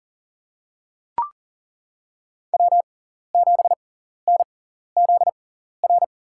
Morse code, heard as a keyed tone, about 37 wpm, spaced as for 13 wpm, 700 Hz. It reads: W7NZR
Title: USA Call Signs - 5 Characters - 6x Character Spacing - 40wpm